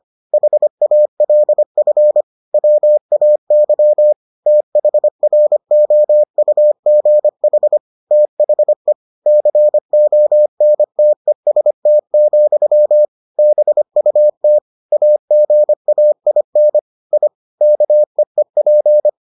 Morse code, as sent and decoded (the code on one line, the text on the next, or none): .... .- .-.. ..-. / .-- .- -.-- / - .... .-. --- ..- --. .... / - .... / -.-. --- -. - . ... - --..-- / -... ..- - / .- --. .- .. -. / .. / -.- . .--.
HALF WAY THROUGH THE CONTEST, BUT AGAIN I KEEP